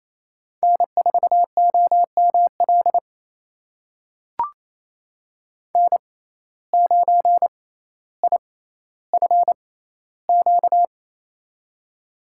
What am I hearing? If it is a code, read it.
Morse code, heard as a keyed tone, about 28 wpm, spaced as for 14 wpm, 700 Hz. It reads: N4OML N9IFQ
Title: USA Call Signs - 5 Characters - 6x Character Spacing - 28wpm